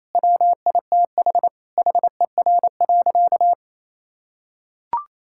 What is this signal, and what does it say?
Morse code, heard as a keyed tone, about 28 wpm, 700 Hz. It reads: WITH HER.
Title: Sentences from Top 1000 Words - Review Part 6 - 28wpm